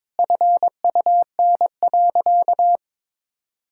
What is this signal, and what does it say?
Morse code, heard as a keyed tone, about 22 wpm, 700 Hz. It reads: FUN.
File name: practice / 22wpm / Sentences with 2-Word Phrases - 22wpm